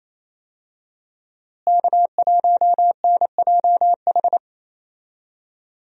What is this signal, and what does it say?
Morse code, heard as a keyed tone, about 27 wpm, 700 Hz. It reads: K1NJH
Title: USA Call Signs - 5 Characters - Repeated 3x - 28wpm